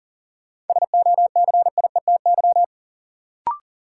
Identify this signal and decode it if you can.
Morse code, heard as a keyed tone, about 40 wpm, 700 Hz. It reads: SOCIETY
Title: Sentences from Top 700 Words - New Word Drill - 40wpm